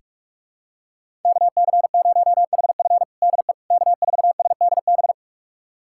Morse code, heard as a keyed tone, about 47 wpm, 700 Hz. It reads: KC0HF DE K4SDB
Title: USA Callsign DE Callsign - 5 Characters - 45wpm